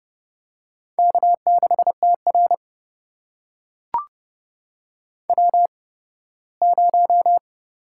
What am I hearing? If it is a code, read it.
Morse code, heard as a keyed tone, about 30 wpm, spaced as for 18 wpm, 700 Hz. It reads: K6TR W0
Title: USA Call Signs - 4 Characters - 8x Character Spacing - 30wpm